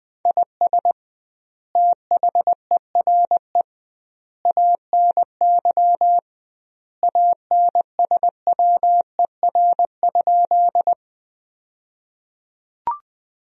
Morse code, heard as a keyed tone, about 20 wpm, 700 Hz. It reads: IS THERE ANY ANSWER?
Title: 2x Word-Spacing - Sentences from Top 300 Words - 20wpm